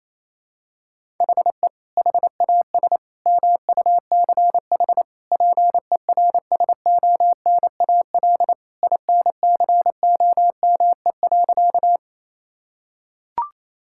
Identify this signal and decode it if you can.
Morse code, heard as a keyed tone, about 27 wpm, 700 Hz. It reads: HE HAS MUCH PERSONAL INCOME.
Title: Sentences from Top 1000 Words - Review Part 5 - 28wpm